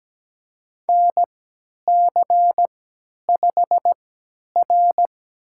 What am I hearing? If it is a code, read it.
Morse code, heard as a keyed tone, about 17 wpm, spaced as for 10 wpm, 700 Hz. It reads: NC5R
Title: USA Call Signs - 4 Characters - 3x Character Spacing - 17wpm